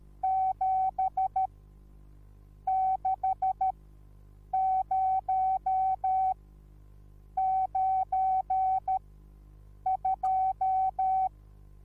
Morse code, heard as a keyed tone, about 13 wpm, spaced as for 7 wpm, 760 Hz. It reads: 7 60 92
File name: Onlinelehrgang / morsen / DJ4UF Morsen21